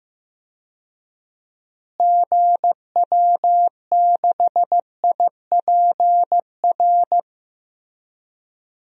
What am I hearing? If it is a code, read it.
Morse code, heard as a keyed tone, about 15 wpm, 700 Hz. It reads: GW6IPR
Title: DXCC-100 Call Signs - 6 Characters - Repeated 3x - 15wpm